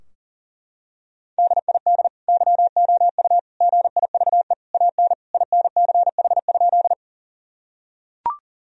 Morse code, heard as a keyed tone, about 40 wpm, 700 Hz. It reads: DID YOU GIVE AN INCH?
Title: Sentences from Top 1000 Words - Review Part 3 - 40wpm